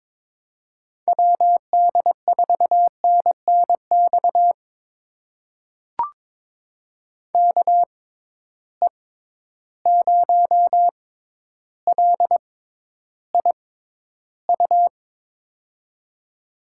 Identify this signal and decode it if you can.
Morse code, heard as a keyed tone, about 23 wpm, spaced as for 11 wpm, 700 Hz. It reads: WD4NNX KE0LIU
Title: USA Call Signs - 6 Characters - 6x Character Spacing - 22wpm